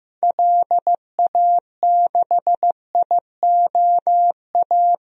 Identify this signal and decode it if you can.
Morse code, heard as a keyed tone, about 15 wpm, 690 Hz. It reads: LA6IOA
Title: DXCC-100 Call Signs - 6 Characters - 15wpm